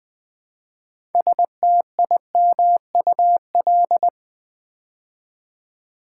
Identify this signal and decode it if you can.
Morse code, heard as a keyed tone, about 20 wpm, 700 Hz. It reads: STIMUL
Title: Build A Word - 3 Letters At A Time - 8-Letter Words - 20wpm